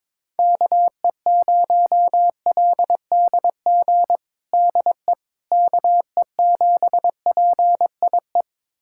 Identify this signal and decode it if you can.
Morse code, heard as a keyed tone, about 22 wpm, 700 Hz. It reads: KE0LDG DE KE7PIE
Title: USA Callsign DE Callsign - 6 Characters - 22wpm